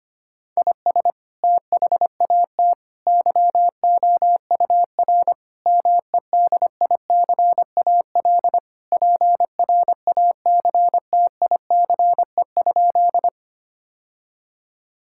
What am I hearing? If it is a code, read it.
Morse code, heard as a keyed tone, about 25 wpm, 700 Hz. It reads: IS THAT YOUR MEDICAL PRACTICE?
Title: Sentences from Top 1000 Words - Review Part 5 - 25wpm